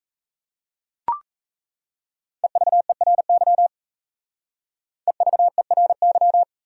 Morse code, heard as a keyed tone, about 36 wpm, 700 Hz. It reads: EVERY EVERY